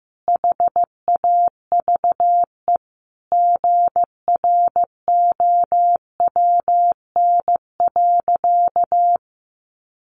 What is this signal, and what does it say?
Morse code, heard as a keyed tone, about 15 wpm, 700 Hz. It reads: HAVE GROWN.
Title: Easy Phrases 15wpm